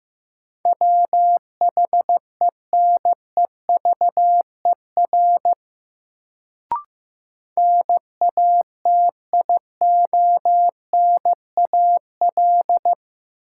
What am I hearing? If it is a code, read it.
Morse code, heard as a keyed tone, about 15 wpm, 700 Hz. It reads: WHENEVER NATIONAL
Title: Sentences from Top 1200 Words - New Word Drill Alt - 15wpm